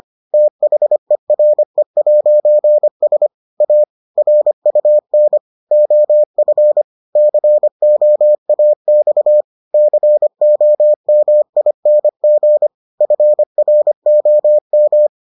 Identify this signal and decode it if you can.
Morse code, heard as a keyed tone, about 25 wpm, 600 Hz. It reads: THERE'S A RUN OF COAX COMING FROM